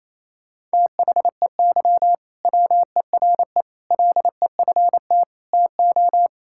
Morse code, heard as a keyed tone, about 28 wpm, 700 Hz. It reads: THEY WERE LEFT TO
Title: Sentences from Top 1000 Words - Review Part 5 - 28wpm